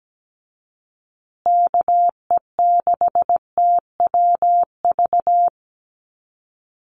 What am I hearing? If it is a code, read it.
Morse code, heard as a keyed tone, about 17 wpm, 700 Hz. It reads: KE6TWV
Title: USA Call Signs Repeated 2x 17wpm